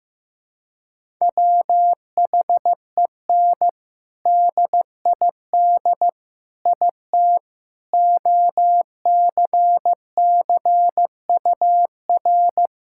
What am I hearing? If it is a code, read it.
Morse code, heard as a keyed tone, about 15 wpm, 700 Hz. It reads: WHEN DID IT OCCUR
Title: Sentences from Top 1000 Words - Review Part 6 - 15wpm